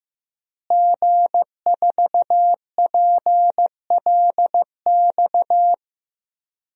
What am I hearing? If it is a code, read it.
Morse code, heard as a keyed tone, about 15 wpm, 700 Hz. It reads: G4PLX
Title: DXCC-100 Call Signs - 5 Characters - Repeated 3x - 15wpm